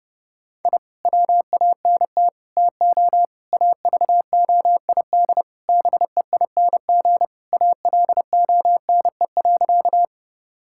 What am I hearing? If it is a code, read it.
Morse code, heard as a keyed tone, about 30 wpm, 700 Hz. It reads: I WANT TO AVOID BEING ALONE.
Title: Sentences from Top 1000 Words - Review Part 6 - 30wpm